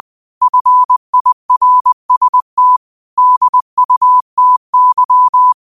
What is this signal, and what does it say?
Morse code, heard as a keyed tone, about 20 wpm, 1000 Hz. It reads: FIRST DUTY
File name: qotdfiles / 20wpm